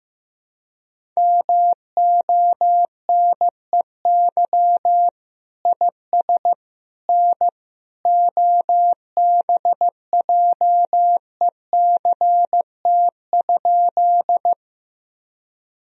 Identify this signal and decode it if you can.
Morse code, heard as a keyed tone, about 16 wpm, 700 Hz. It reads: MONEY IS N OBJECT?
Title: Sentences from Top 500 Words and Abbreviations 15wpm